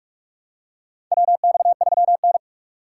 Morse code, heard as a keyed tone, about 42 wpm, 700 Hz. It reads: WX3N
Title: USA Call Signs - 3x Character Spacing - 45wpm